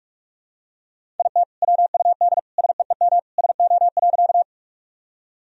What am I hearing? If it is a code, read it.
Morse code, heard as a keyed tone, about 46 wpm, 700 Hz. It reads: IT WUD SEEM SO.
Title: Sentences from Top 300 Words and Abbreviations 45wpm